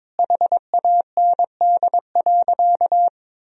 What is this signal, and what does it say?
Morse code, heard as a keyed tone, about 22 wpm, 700 Hz. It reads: HAND.
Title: Sentences from Top 400 Words and Abbreviations 22wpm